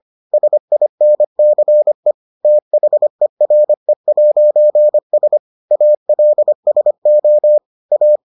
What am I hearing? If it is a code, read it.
Morse code, heard as a keyed tone, about 25 wpm, 600 Hz. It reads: SINCE THERE'S ALSO A